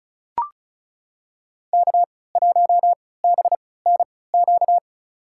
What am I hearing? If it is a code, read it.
Morse code, heard as a keyed tone, about 36 wpm, spaced as for 20 wpm, 700 Hz. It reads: K1BNQ
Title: USA Call Signs - 3x Character Spacing - 35wpm